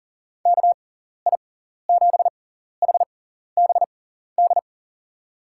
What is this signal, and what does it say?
Morse code, heard as a keyed tone, about 40 wpm, spaced as for 14 wpm, 700 Hz. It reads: KI7HBD
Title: USA Call Signs - 6 Characters - 6x Character Spacing - 40wpm